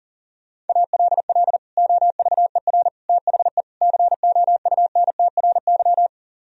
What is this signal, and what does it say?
Morse code, heard as a keyed tone, about 48 wpm, 700 Hz. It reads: ALL OVER THE COUNTRY
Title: Speed-Racing - Sets of 4 Words 40wpm